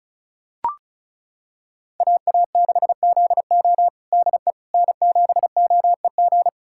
Morse code, heard as a keyed tone, about 34 wpm, 700 Hz. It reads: AA6ZO DE N7OEG
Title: USA Callsign DE Callsign - 5 Characters - 35wpm